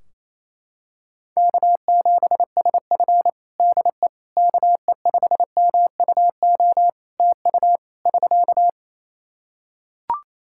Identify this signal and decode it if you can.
Morse code, heard as a keyed tone, about 27 wpm, 700 Hz. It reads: K7SF DE KE5MUO TU <SK>
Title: K1USN SST QSO Practice 28wpm